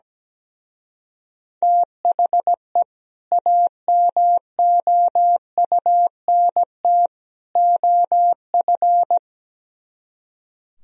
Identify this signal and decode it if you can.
Morse code, heard as a keyed tone, about 17 wpm, 700 Hz. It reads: THE AMOUNT OF
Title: Speed-Racing - Sets of 3 Words 17wpm